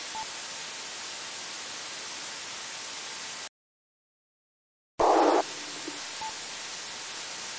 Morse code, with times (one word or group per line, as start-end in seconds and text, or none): none